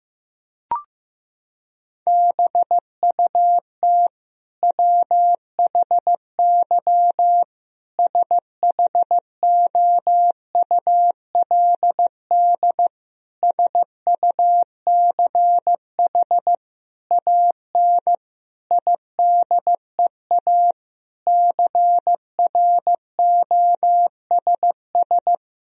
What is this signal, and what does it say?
Morse code, heard as a keyed tone, about 16 wpm, 700 Hz. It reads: BUT WHY SHOULD SUCH AN IDEA CROSS
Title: Sentences from Top 1000 Words - Review Part 1 - 15wpm